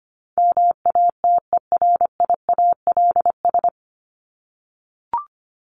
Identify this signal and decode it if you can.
Morse code, heard as a keyed tone, about 25 wpm, 700 Hz. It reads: MATERIALS
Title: ES:S Suffix - Top 100 Words with Base Words 25wpm